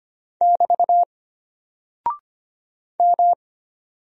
Morse code, heard as a keyed tone, about 25 wpm, 700 Hz. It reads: <BT> M
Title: Single Character (Letter:Number:Punctuation:Prosign) Alt 25wpm